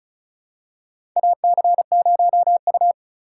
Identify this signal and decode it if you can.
Morse code, heard as a keyed tone, about 36 wpm, 700 Hz. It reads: AC0U
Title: USA Call Signs - 2x Character Spacing - 35wpm